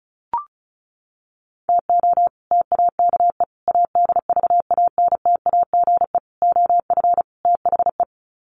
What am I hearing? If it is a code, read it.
Morse code, heard as a keyed tone, about 34 wpm, 700 Hz. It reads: TO TAKE ADVANTAGE OF THE
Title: Sets of 5 Words 35wpm